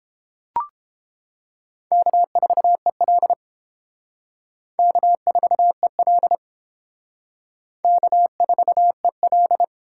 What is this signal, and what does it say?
Morse code, heard as a keyed tone, about 30 wpm, 700 Hz. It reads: K4EL K4EL K4EL